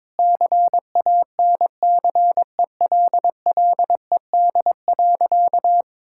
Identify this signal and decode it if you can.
Morse code, heard as a keyed tone, about 22 wpm, 700 Hz. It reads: CANCELLED.
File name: practice / 22wpm / Sentences with 8-Word Phrases - 22wpm